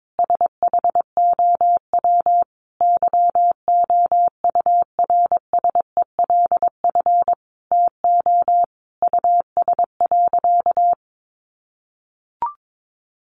Morse code, heard as a keyed tone, about 22 wpm, 700 Hz. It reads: SHOW YOURSELF TO US.
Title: Sentences from Top 600 Words 22wpm